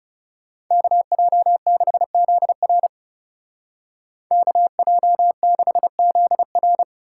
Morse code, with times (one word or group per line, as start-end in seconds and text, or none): 0.7-2.9 KJ6ZR
4.3-6.8 KJ6ZR